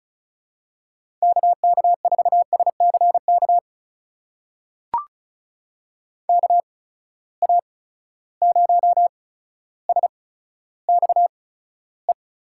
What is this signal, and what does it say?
Morse code, heard as a keyed tone, about 36 wpm, spaced as for 15 wpm, 700 Hz. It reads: KK4SCK KA0SXE
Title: USA Call Signs - 6 Characters - 8x Character Spacing - 35wpm